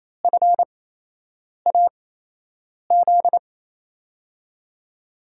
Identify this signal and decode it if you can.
Morse code, heard as a keyed tone, about 28 wpm, spaced as for 8 wpm, 700 Hz. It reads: FAZ
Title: USA Call Signs - 8x Character Spacing - 28wpm